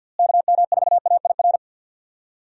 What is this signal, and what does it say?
Morse code, heard as a keyed tone, about 46 wpm, 700 Hz. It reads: KM4AIR
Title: USA Call Signs - 6 Characters - Repeated 2x - 50wpm